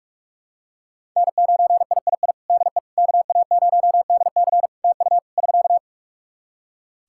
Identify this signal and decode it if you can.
Morse code, heard as a keyed tone, about 46 wpm, 700 Hz. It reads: N9III DE KA0DC TU <SK>
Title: Practice QSO - Exchange 4 - Ending - 45wpm